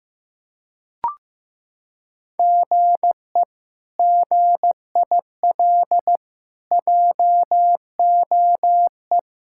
Morse code, heard as a keyed tone, about 15 wpm, 700 Hz. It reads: GE GIL JOE
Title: K1USN SST QSO Practice - 15wpm